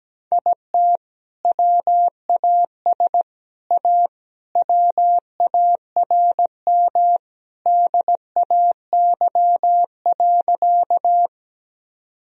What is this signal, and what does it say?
Morse code, heard as a keyed tone, about 17 wpm, 700 Hz. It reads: IT WAS A WARM DAY.
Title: Sentences from Top 1000 Words - Review Part 2 - 17wpm